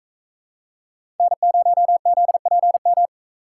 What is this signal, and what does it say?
Morse code, heard as a keyed tone, about 42 wpm, 700 Hz. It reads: N0ZPM